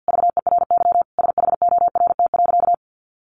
Morse code, heard as a keyed tone, about 50 wpm, 700 Hz. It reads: VERY SHORT.
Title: Sentences from Top 1400 Words 50wpm